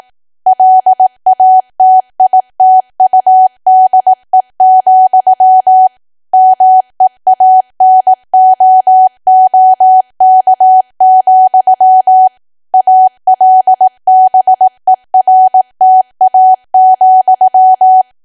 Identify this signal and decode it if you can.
Morse code, heard as a keyed tone, about 18 wpm, 750 Hz. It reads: LATITUDE, MEANOOK, ALBERTA,
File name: cwa / W1AW-Level2 / 18WPM